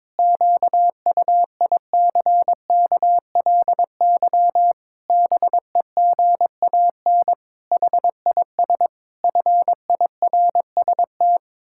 Morse code, heard as a keyed tone, about 22 wpm, 700 Hz. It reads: QUICKLY BEGAN HIS FIRST
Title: Pangram Challenge v2 - 22wpm